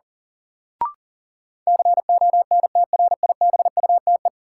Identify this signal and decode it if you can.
Morse code, heard as a keyed tone, about 37 wpm, 700 Hz. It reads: CONTRIBUTE
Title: Sentences from Top 1400 Words - New Word Drill Alt - 40wpm